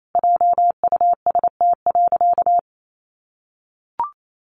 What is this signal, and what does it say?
Morse code, heard as a keyed tone, about 28 wpm, 700 Hz. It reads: JUST.
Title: Sentences from Top 1400 Words 28wpm